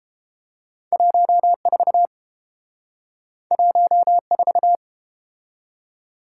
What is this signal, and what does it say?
Morse code, heard as a keyed tone, about 30 wpm, 700 Hz. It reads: 14 14